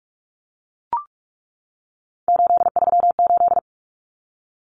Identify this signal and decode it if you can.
Morse code, heard as a keyed tone, about 44 wpm, 700 Hz. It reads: 838
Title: Numbers - 3-Digit 45wpm